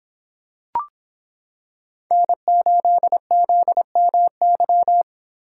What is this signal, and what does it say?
Morse code, heard as a keyed tone, about 26 wpm, 700 Hz. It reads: N8ZMY